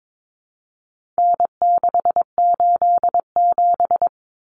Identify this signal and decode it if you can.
Morse code, heard as a keyed tone, about 22 wpm, 700 Hz. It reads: N687
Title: Common Cut Numbers - 4-Digit 22wpm